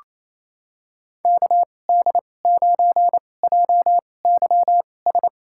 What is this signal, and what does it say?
Morse code, heard as a keyed tone, about 28 wpm, spaced as for 20 wpm, 700 Hz. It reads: KD9JYS
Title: USA Call Signs - 6 Characters - 2x Character Spacing - 28wpm